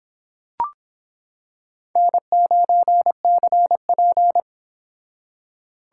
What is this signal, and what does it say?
Morse code, heard as a keyed tone, about 26 wpm, 700 Hz. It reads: N9CP